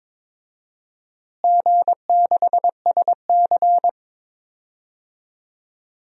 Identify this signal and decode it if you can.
Morse code, heard as a keyed tone, about 21 wpm, 700 Hz. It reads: G6SC
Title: DXCC-100 Call Signs - 4 Characters - Repeated 3x - 22wpm